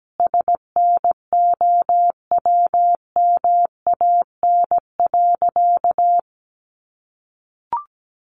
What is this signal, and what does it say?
Morse code, heard as a keyed tone, about 17 wpm, 700 Hz. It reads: SNOWMAN.
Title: Easy Phrases 17wpm